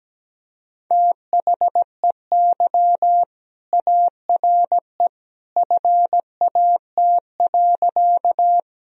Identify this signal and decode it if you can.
Morse code, heard as a keyed tone, about 17 wpm, 700 Hz. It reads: THEY ARE FAT.
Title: Sentences from Top 1000 Words - Review Part 4 - 17wpm